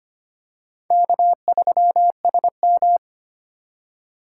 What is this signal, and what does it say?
Morse code, heard as a keyed tone, about 25 wpm, 700 Hz. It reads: K3SM
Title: USA Call Signs - 4x Character Spacing - 25wpm